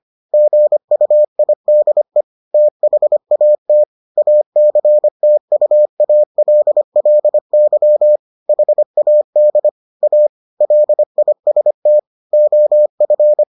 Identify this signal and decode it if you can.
Morse code, heard as a keyed tone, about 25 wpm, 600 Hz. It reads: GUIDE THAT ACTUALLY HAD A LIST OF